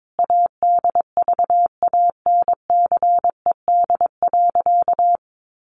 Morse code, heard as a keyed tone, about 22 wpm, 700 Hz. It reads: ADVANCED.
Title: Sentences from Top 1400 Words 22wpm